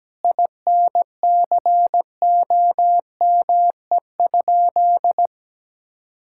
Morse code, heard as a keyed tone, about 17 wpm, 700 Hz. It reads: INCOME?
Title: Sentences from Top 1000 Words - Review Part 6 - 17wpm